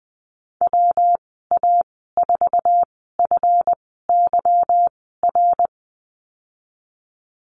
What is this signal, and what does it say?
Morse code, heard as a keyed tone, about 20 wpm, spaced as for 14 wpm, 700 Hz. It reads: WA4FYR
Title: 186 Lesson 30g - Letter b - Call Signs 20wpm